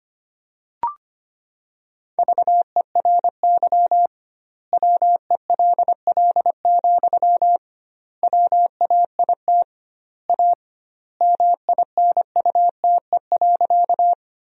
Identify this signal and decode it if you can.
Morse code, heard as a keyed tone, about 25 wpm, 700 Hz. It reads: VERY WELL, WAIT A MINUTE.
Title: 2x Word-Spacing - Sentences from Top 500 Words - 25wpm